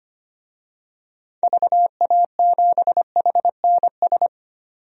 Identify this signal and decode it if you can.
Morse code, heard as a keyed tone, about 25 wpm, 700 Hz. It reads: VA7HNS
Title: DXCC-100 Call Signs - 6 Characters - 25wpm